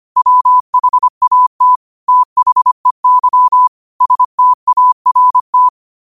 Morse code, heard as a keyed tone, about 25 wpm, 1000 Hz. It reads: WHAT THEY START
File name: qotdfiles / 25wpm